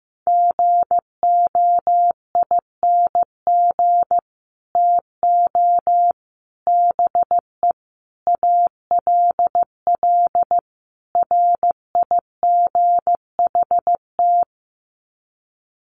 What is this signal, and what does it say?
Morse code, heard as a keyed tone, about 15 wpm, 700 Hz. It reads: GOING TO BE ALL RIGHT
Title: Sets of 5 Words Encore2 15wpm